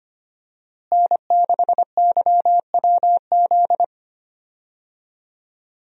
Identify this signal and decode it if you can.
Morse code, heard as a keyed tone, about 25 wpm, 700 Hz. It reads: N6YWZ
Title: USA Call Signs - 5 Characters - Repeated 2x - 25wpm